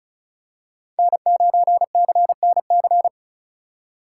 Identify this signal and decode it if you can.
Morse code, heard as a keyed tone, about 36 wpm, 700 Hz. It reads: N9CNC
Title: USA Call Signs - 4x Character Spacing - 35wpm